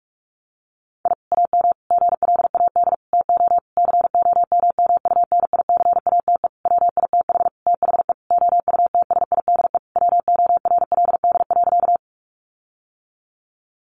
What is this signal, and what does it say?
Morse code, heard as a keyed tone, about 44 wpm, 700 Hz. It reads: I AM GLAD TO COMMUNICATE WITH THE OUTSIDE WORLD.
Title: Sentences from Top 1300 Words 45wpm